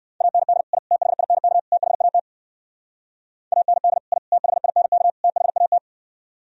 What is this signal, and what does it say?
Morse code, heard as a keyed tone, about 58 wpm, 700 Hz. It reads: AND I THINK THAT AND I THINK THAT